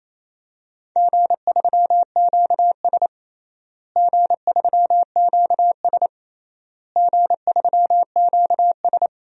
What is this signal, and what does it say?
Morse code, heard as a keyed tone, about 28 wpm, 700 Hz. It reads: G3QS G3QS G3QS
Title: DXCC-100 Call Signs - 4 Characters - Repeated 3x - 28wpm